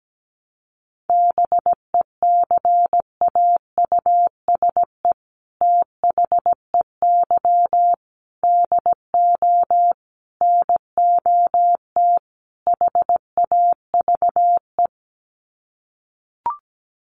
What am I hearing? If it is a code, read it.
Morse code, heard as a keyed tone, about 17 wpm, 700 Hz. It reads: BECAUSE THEY DO NOT HAVE
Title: Sets of 5 Words Encore2 17wpm